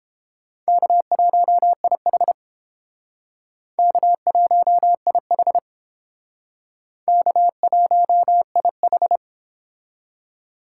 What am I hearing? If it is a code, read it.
Morse code, heard as a keyed tone, about 29 wpm, 700 Hz. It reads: K1IH K1IH K1IH